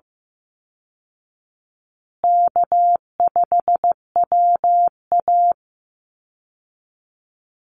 Morse code, heard as a keyed tone, about 15 wpm, 700 Hz. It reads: K5WA
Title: Easy Call Signs Repeated 3x - 15wpm